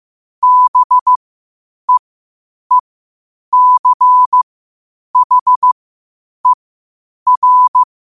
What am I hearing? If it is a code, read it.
Morse code, heard as a keyed tone, about 15 wpm, spaced as for 8 wpm, 1000 Hz. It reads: BEECHER
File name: qotdfiles / 8wpm